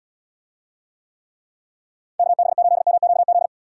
Morse code, heard as a keyed tone, about 78 wpm, 700 Hz. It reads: KB8MQZ